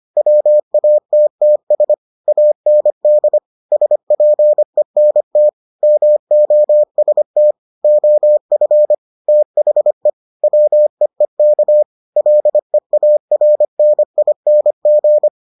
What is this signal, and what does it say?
Morse code, heard as a keyed tone, about 25 wpm, 600 Hz. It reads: WATTS AND SPENT MOST OF THE WEEK LEARNING